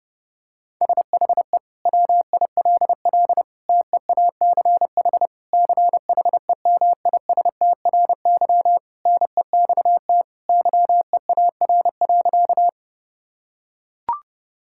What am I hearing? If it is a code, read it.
Morse code, heard as a keyed tone, about 30 wpm, 700 Hz. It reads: SHE WILL TEACH CHEMISTRY NEXT YEAR.
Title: Sentences from Top 1000 Words - Review Part 6 - 30wpm